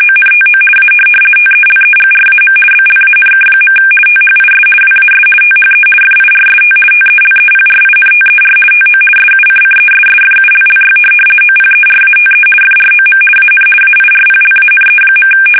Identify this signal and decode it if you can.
RTTY, about 50 baud. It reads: ONTO A TELEGRAM BLANK. THIS WAS THE FIRST TELEPRINTER TO SUCCESSFULLY OPERATE FROM AN AIRPLANE.
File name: fsk850-50B / teletype-corp